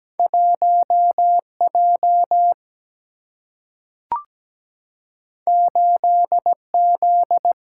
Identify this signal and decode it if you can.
Morse code, heard as a keyed tone, about 17 wpm, 700 Hz. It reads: 1 J 8 Z
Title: Single-Letter Single-Number 17wpm